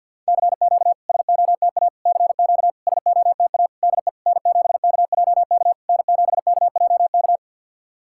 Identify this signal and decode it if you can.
Morse code, heard as a keyed tone, about 52 wpm, 690 Hz. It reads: CQ SOTA CQ SOTA DE N7KJK N7KJK
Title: SOTA QSO Practice 50wpm